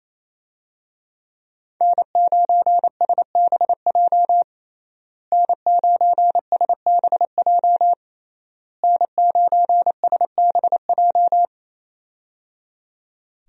Morse code, heard as a keyed tone, about 27 wpm, 700 Hz. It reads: N9SBJ N9SBJ N9SBJ
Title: USA Call Signs - 5 Characters - Repeated 3x - 28wpm